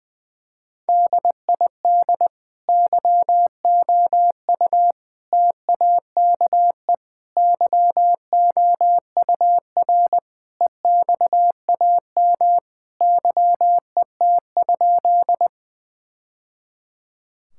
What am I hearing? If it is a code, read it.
Morse code, heard as a keyed tone, about 21 wpm, 700 Hz. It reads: DID YOU TAKE YOUR EXAM YET?
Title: Sentences from Top 1000 Words - Review Part 5 - 20wpm